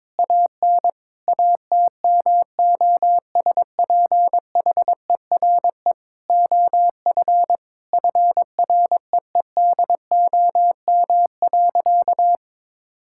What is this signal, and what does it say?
Morse code, heard as a keyed tone, about 22 wpm, 700 Hz. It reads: AN ATMOSPHERE OF FREEDOM.
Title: Sentences from Top 1600 Words 22wpm